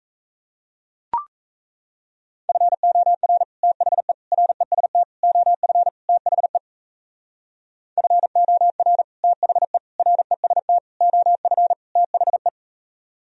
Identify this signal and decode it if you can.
Morse code, heard as a keyed tone, about 34 wpm, 700 Hz. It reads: FOR THE REST OF THE FOR THE REST OF THE
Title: Speed-Racing - Sets of 5 Words 28wpm